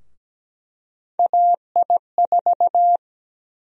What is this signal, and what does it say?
Morse code, heard as a keyed tone, about 17 wpm, 700 Hz. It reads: AI4
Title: USA Call Sign Prefixes - Repeated 2x - 17wpm